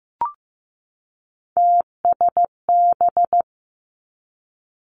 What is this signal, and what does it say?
Morse code, heard as a keyed tone, about 15 wpm, 700 Hz. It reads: TSB
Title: Three Challenging Letter-Number Combinations 15wpm